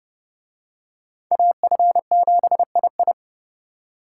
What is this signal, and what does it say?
Morse code, heard as a keyed tone, about 29 wpm, 700 Hz. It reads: AF7II